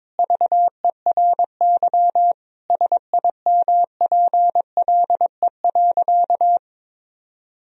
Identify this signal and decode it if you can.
Morse code, heard as a keyed tone, about 22 wpm, 700 Hz. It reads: VERY SIMPLE.
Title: Sentences from Top 1000 Words - Review Part 4 - 22wpm